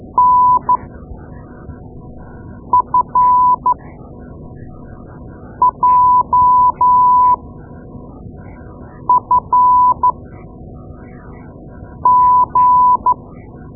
Morse code, heard as a keyed tone, about 13 wpm, spaced as for 4 wpm, 1000 Hz. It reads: NFJFG
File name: Morse Lessons / Morse Lessons / Tape-2A